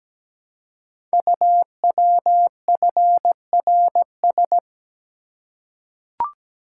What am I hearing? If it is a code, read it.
Morse code, heard as a keyed tone, about 17 wpm, 700 Hz. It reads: UWFRS
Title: Code Groups - 5 Letters - 17wpm